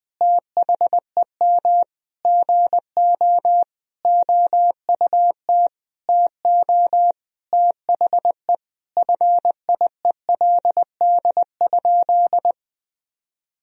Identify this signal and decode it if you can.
Morse code, heard as a keyed tone, about 20 wpm, 700 Hz. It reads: THEM GO OUT TO THE FIELD?
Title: Sentences from Top 1000 Words - Review Part 2 - 20wpm